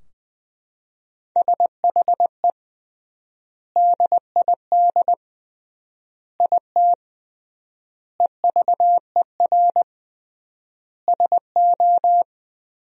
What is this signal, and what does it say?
Morse code, heard as a keyed tone, about 21 wpm, 700 Hz. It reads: SHE DID IT EVER SO
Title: 3x Word-Spacing - Sentences from Top 400 Words - 20wpm